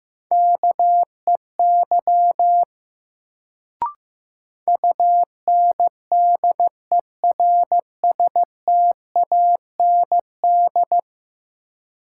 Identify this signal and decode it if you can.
Morse code, heard as a keyed tone, about 16 wpm, 700 Hz. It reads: KEY UNDERSTAND
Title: Sentences from Top 600 Words - New Word Drill - 15wpm